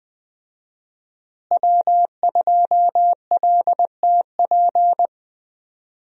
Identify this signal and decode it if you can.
Morse code, heard as a keyed tone, about 20 wpm, 700 Hz. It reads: W2LTP
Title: USA Call Signs - 5 Characters - Repeated 3x - 20wpm